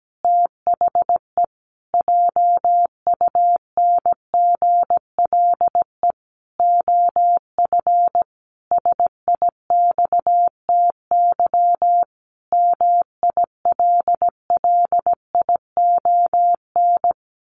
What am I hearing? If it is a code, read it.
Morse code, heard as a keyed tone, about 17 wpm, 700 Hz. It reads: THE JUNGLE OF SIXTY MILLION
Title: A Sound Of Thunder 17wpm